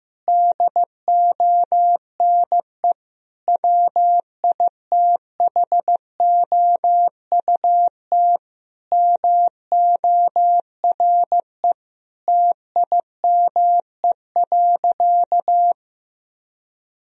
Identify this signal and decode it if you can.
Morse code, heard as a keyed tone, about 16 wpm, 700 Hz. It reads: DONE WITHOUT MORE TIME.
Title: Sentences from Top 1000 Words - Review Part 3 - 15wpm